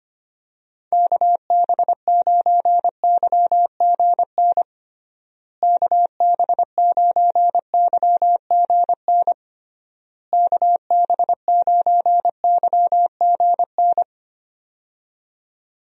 Morse code, heard as a keyed tone, about 25 wpm, 700 Hz. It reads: KB9YGN KB9YGN KB9YGN
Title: USA Call Signs - 6 Characters - Repeated 3x - 25wpm